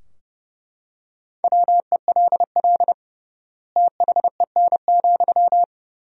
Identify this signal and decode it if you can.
Morse code, heard as a keyed tone, about 32 wpm, 700 Hz. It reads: WELL THEN,
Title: 3x Word-Spacing - Sentences from Top 200 Words - 30wpm